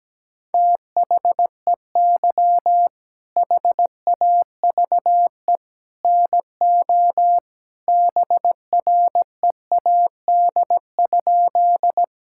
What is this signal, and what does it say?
Morse code, heard as a keyed tone, about 17 wpm, 700 Hz. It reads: THEY HAVE NO BREAD?
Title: Sentences from Top 1600 Words 17wpm